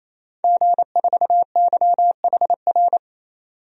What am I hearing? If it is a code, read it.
Morse code, heard as a keyed tone, about 28 wpm, 700 Hz. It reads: G4YHR
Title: DXCC-100 Call Signs - 5 Characters - 28wpm